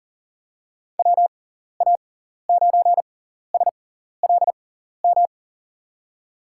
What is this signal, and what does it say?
Morse code, heard as a keyed tone, about 40 wpm, spaced as for 14 wpm, 700 Hz. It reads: WA9SLM
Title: USA Call Signs - 6 Characters - 6x Character Spacing - 40wpm